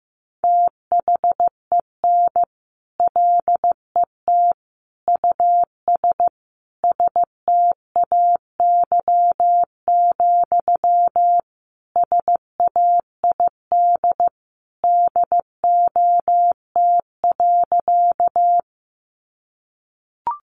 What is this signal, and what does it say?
Morse code, heard as a keyed tone, about 15 wpm, 700 Hz. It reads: THEN LET US STAY, SAID DOT.
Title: Sentences from Top 1500 Words 15wpm